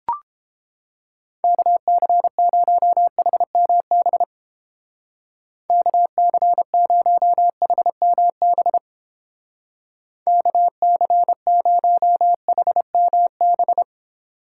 Morse code, heard as a keyed tone, about 30 wpm, 700 Hz. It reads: KC0HMB KC0HMB KC0HMB